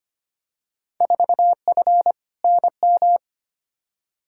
Morse code, heard as a keyed tone, about 25 wpm, 700 Hz. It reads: 4F NM
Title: Field Day Exchanges Encore - 25wpm